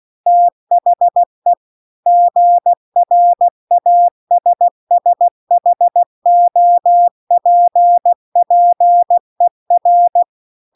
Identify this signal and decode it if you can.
Morse code, heard as a keyed tone, about 16 wpm, 700 Hz. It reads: THE GRASSHOPPER